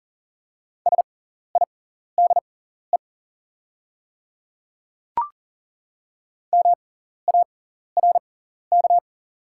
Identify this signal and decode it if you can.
Morse code, heard as a keyed tone, about 40 wpm, spaced as for 14 wpm, 690 Hz. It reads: SIDE MARK
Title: 6x Character Spacing - Top 500 Words - 40wpm